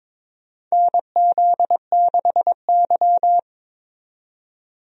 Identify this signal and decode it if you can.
Morse code, heard as a keyed tone, about 22 wpm, 700 Hz. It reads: NZ6Y
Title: USA Call Signs - 4 Characters - 22wpm